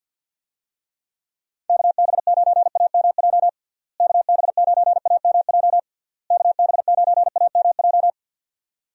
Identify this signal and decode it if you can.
Morse code, heard as a keyed tone, about 51 wpm, 700 Hz. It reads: KB9AMJ KB9AMJ KB9AMJ
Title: USA Call Signs - 6 Characters - Repeated 3x - 50wpm